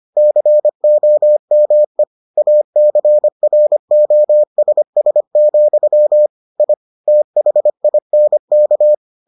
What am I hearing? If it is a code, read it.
Morse code, heard as a keyed tone, about 25 wpm, 600 Hz. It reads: COME ACROSS, I THINK